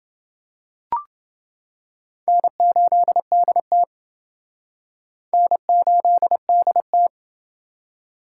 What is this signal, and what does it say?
Morse code, heard as a keyed tone, about 28 wpm, 700 Hz. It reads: N8DT N8DT